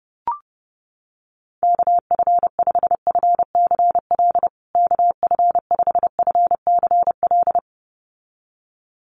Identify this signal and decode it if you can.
Morse code, heard as a keyed tone, about 30 wpm, 700 Hz. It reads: KF5FCL KF5FCL
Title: USA Call Signs Repeated 2x 30wpm